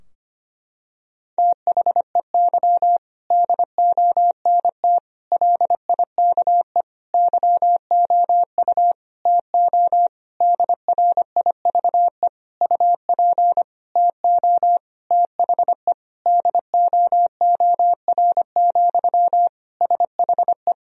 Morse code, heard as a keyed tone, about 25 wpm, 700 Hz. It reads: THEY DONT LIKE YOU TO DRIVE UP TO THE DOOR, SHE
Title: Sentences from Top 1000 Words - Review Part 2 - 25wpm